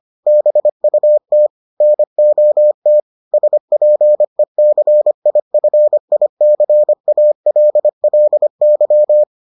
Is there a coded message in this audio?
Yes: Morse code from 0.3 to 9.2 s, heard as a keyed tone, about 25 wpm, 600 Hz. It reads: BUT NOT SPECIFICALLY